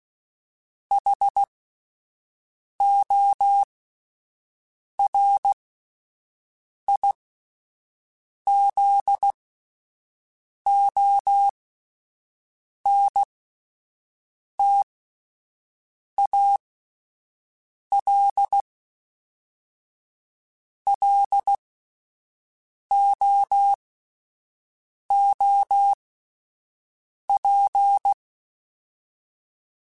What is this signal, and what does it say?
Morse code, heard as a keyed tone, about 16 wpm, spaced as for 5 wpm, 800 Hz. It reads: HORIZONTAL LOOP
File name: fromSMs / FromSMs mp3s / Antennas@5